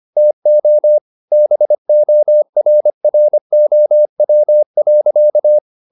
Morse code, heard as a keyed tone, about 25 wpm, 600 Hz. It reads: TO BORROW.
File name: the old man and the sea part 1 25wpm